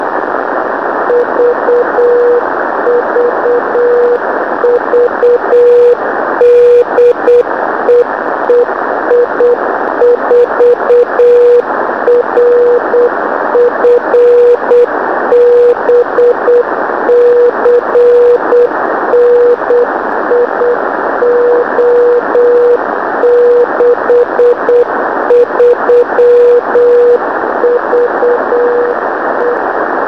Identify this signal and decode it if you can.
Morse code, heard as a keyed tone, about 8 wpm, 480 Hz, with radio noise. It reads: VVV DE EI4RF B CN IO63VE
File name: ei4rf b 4m